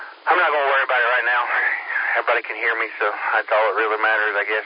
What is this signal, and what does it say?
Emergency Alert System (SAME) tones, heard as harsh buzzing bursts, with speech. It is heard from 0.8 to 4.5 s.